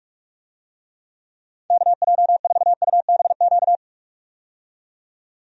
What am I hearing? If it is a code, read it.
Morse code, heard as a keyed tone, about 41 wpm, 700 Hz. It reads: KJ4UBQ
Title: USA Call Signs - 6 Characters - 45wpm